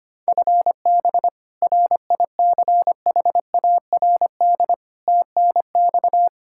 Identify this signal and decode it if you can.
Morse code, heard as a keyed tone, about 25 wpm, 700 Hz. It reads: FB RICHARD TNX
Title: Practice QSO - Exchange 3 - More Info - 25wpm